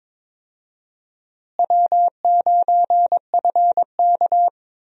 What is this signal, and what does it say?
Morse code, heard as a keyed tone, about 21 wpm, 700 Hz. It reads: W9FK
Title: USA Call Signs - 4 Characters - Repeated 2x - 22wpm